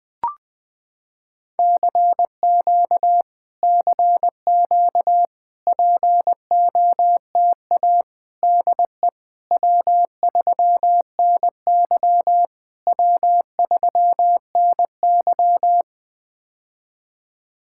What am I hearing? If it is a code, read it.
Morse code, heard as a keyed tone, about 20 wpm, 700 Hz. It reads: CQ CQ POTA DE W3NY W3NY
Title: POTA QSO Practice - 20wpm